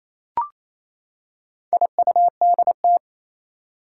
Morse code, heard as a keyed tone, about 28 wpm, 700 Hz. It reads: IUDT
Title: Four Challenging Letter-Number Combinations 28wpm